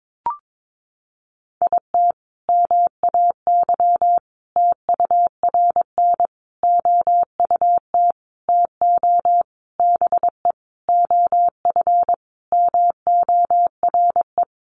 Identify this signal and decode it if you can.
Morse code, heard as a keyed tone, about 22 wpm, 700 Hz. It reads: IT MAY TURN OUT TO BE OF MORE
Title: Sentences from Top 400 Words 22wpm